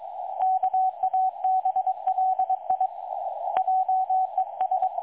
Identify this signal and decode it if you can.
Morse code, heard as a keyed tone, about 22 wpm, 750 Hz, with radio noise. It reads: KABLI JES
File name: dlazaawansowanych / nagraniazpasma / websdr recording 3565.0kHz